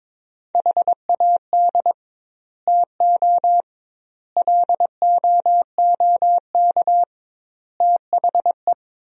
Morse code, heard as a keyed tone, about 22 wpm, 700 Hz. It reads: HAD TO LOOK THE
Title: 2x Word-Spacing - Sentences from Top 100 Words - 22wpm